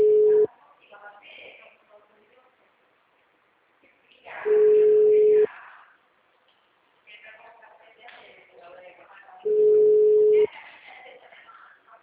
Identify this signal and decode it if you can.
Ringback tone (Europe, 425 Hz), with speech.
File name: out- -2045- - - .54116